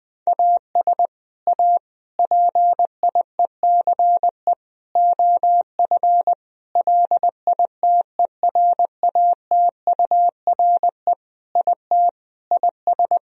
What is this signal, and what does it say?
Morse code, heard as a keyed tone, about 21 wpm, 700 Hz. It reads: AS A PIECE OF LITERATURE IT IS
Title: Sentences from Top 1000 Words - Review Part 3 - 20wpm